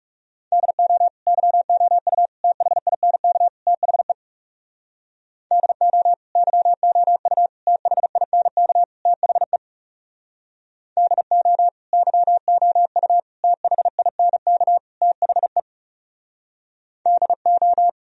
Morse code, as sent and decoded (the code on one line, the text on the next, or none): -.. --- / -.-- --- ..- / - .... .. -. -.- / - .... / -.. --- / -.-- --- ..- / - .... .. -. -.- / - .... / -.. --- / -.-- --- ..- / - .... .. -. -.- / - .... / -.. ---
DO YOU THINK THE DO YOU THINK THE DO YOU THINK THE DO